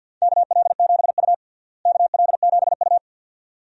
Morse code, heard as a keyed tone, about 50 wpm, 700 Hz. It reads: KL7U KL7U
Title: DXCC-100 Call Signs - 4 Characters - Repeated 2x - 50wpm